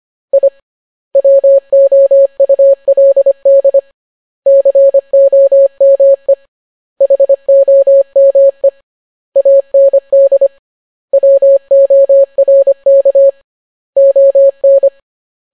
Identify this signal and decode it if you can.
Morse code, heard as a keyed tone, about 25 wpm, 550 Hz. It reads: I WOULD COME HOME AND WORK ON